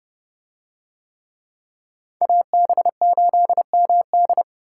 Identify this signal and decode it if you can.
Morse code, heard as a keyed tone, about 28 wpm, 700 Hz. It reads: AB8MD